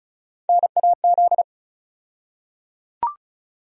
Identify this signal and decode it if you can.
Morse code, heard as a keyed tone, about 36 wpm, 700 Hz. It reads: NAZ
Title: USA Call Sign Postfixes - 3 Letters 35wpm